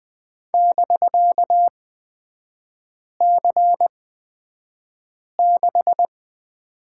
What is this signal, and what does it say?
Morse code, heard as a keyed tone, about 20 wpm, spaced as for 5 wpm, 700 Hz. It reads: <BK>C6
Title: 250 Lesson 40e - BK - Three Characters 20wpm